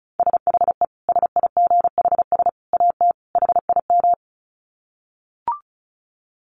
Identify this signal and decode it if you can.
Morse code, heard as a keyed tone, about 34 wpm, 700 Hz. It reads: SHE SIGHS AT HIM
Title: Dit Challenge 35wpm